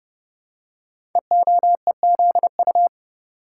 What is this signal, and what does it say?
Morse code, heard as a keyed tone, about 30 wpm, 700 Hz. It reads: EOEZU
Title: Code Groups - 5 Letters - 30wpm